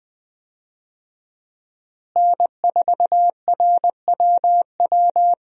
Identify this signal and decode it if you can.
Morse code, heard as a keyed tone, about 20 wpm, 700 Hz. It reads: N4RWW
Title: USA Call Signs - 5 Characters - Repeated 2x - 20wpm